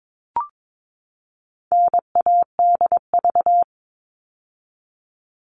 Morse code, heard as a keyed tone, about 22 wpm, 700 Hz. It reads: NADV
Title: All Cut Numbers - 4-Digit 22wpm